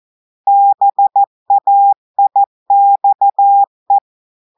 Morse code, heard as a keyed tone, about 14 wpm, 800 Hz. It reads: BAIXE
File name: audios / portugais / portugais texte en clair 2 vitesse 14 mots minutes